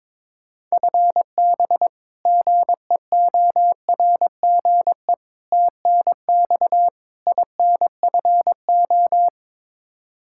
Morse code, heard as a keyed tone, about 22 wpm, 700 Hz. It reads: FB GEORGE TNX INFO
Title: K1USN SST QSO Practice 22wpm